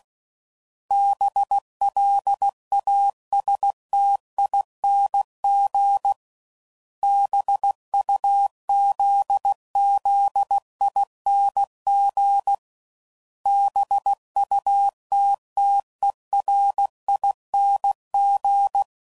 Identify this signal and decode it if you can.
Morse code, heard as a keyed tone, about 16 wpm, 800 Hz. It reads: BLASTING BUZZING BUTTERING
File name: Suffixes / Special INGs / Sngl Ings